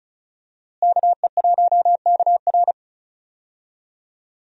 Morse code, heard as a keyed tone, about 34 wpm, 700 Hz. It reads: KE1KR
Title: USA Call Signs - 5 Characters - Repeated 2x - 35wpm